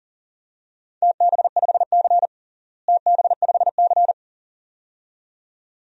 Morse code, heard as a keyed tone, about 37 wpm, 700 Hz. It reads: TB5C TB5C
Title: DXCC-100 Call Signs - 4 Characters - Repeated 2x - 40wpm